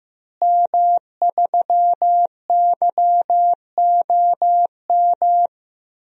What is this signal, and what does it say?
Morse code, heard as a keyed tone, about 15 wpm, 700 Hz. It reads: M3YOM
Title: DXCC-100 Call Signs - 5 Characters - Repeated 3x - 15wpm